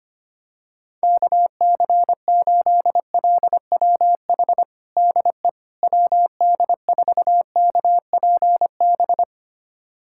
Morse code, heard as a keyed tone, about 25 wpm, 700 Hz. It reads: KC8LWH DE WD4KPB
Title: USA Callsign DE Callsign 25wpm